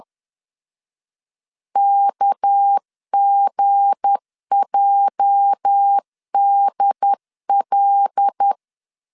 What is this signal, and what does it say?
Morse code, heard as a keyed tone, about 10 wpm, 800 Hz. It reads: KGJDL
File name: CW / DET / 1 9 1500C S B 10wpm 4439